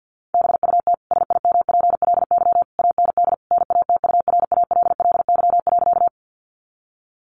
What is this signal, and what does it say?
Morse code, heard as a keyed tone, about 50 wpm, 700 Hz. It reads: BUT SIMPLY AND NATURALLY.
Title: Sentences from Top 1300 Words 50wpm